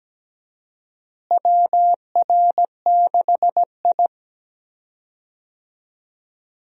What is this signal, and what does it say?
Morse code, heard as a keyed tone, about 17 wpm, 700 Hz. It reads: WR6I
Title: USA Call Signs - 4 Characters - Repeated 2x - 17wpm